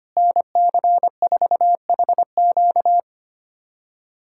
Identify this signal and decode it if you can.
Morse code, heard as a keyed tone, about 25 wpm, 700 Hz. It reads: NC4HQ
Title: USA Callsign DE Callsign - 5 Characters - 25wpm